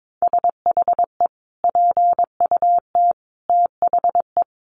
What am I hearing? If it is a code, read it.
Morse code, heard as a keyed tone, about 22 wpm, 700 Hz. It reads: SHE PUT THE
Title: Sentences from Top 1400 Words 22wpm